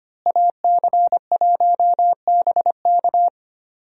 Morse code, heard as a keyed tone, about 25 wpm, 700 Hz. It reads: AC1BK
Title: USA Call Signs - 5 Characters - Repeated 3x - 25wpm